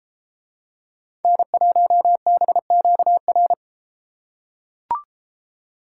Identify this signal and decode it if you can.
Morse code, heard as a keyed tone, about 33 wpm, 700 Hz. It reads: N1BQR